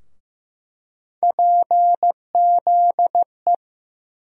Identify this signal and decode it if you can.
Morse code, heard as a keyed tone, about 15 wpm, 700 Hz. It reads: PZE
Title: USA Call Sign Postfixes - 3 Letters 15wpm